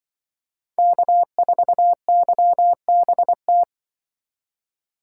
Morse code, heard as a keyed tone, about 24 wpm, 700 Hz. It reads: K4YBT
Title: Speed Racing - USA 1x3 Call Signs 20wpm